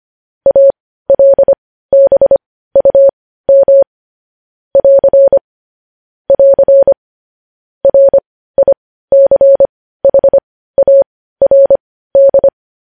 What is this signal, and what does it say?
Morse code, heard as a keyed tone, about 25 wpm, spaced as for 15 wpm, 550 Hz. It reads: ALBUM <AR> <AR> RICHARD
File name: podcast / News headlines-1x-25x15